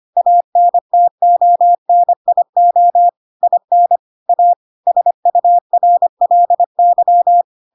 Morse code, heard as a keyed tone, about 25 wpm, 700 Hz. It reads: ANTONIO IN A SURLY
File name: Pinocchio / Pinocchio ch2 25 25 ik1hge